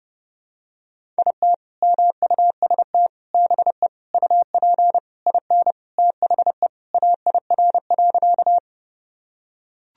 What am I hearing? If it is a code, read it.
Morse code, heard as a keyed tone, about 29 wpm, 700 Hz. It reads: IT MUST BE UP IN THE AIR.
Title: Sentences from Top 1000 Words - Review Part 1 - 30wpm